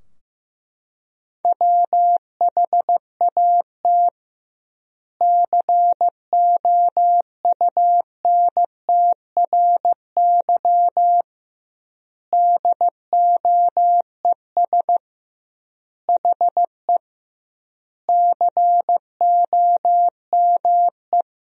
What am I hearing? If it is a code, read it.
Morse code, heard as a keyed tone, about 16 wpm, 700 Hz. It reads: WHAT COUNTRY DOES HE COME
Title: 2x Word-Spacing - Sentences from Top 300 Words - 15wpm